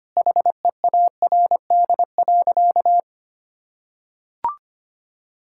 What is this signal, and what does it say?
Morse code, heard as a keyed tone, about 25 wpm, 700 Hz. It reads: HEARD.
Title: Sentences from Top 1600 Words 25wpm